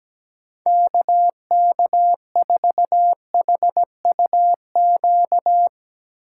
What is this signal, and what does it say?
Morse code, heard as a keyed tone, about 17 wpm, 700 Hz. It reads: KK4HUQ
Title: USA Call Signs - 6 Characters - 4x Character Spacing - 17wpm